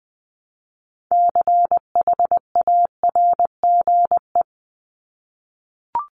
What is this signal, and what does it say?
Morse code, heard as a keyed tone, about 20 wpm, 700 Hz. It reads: CHARGE
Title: Top 751-1000 Words 20wpm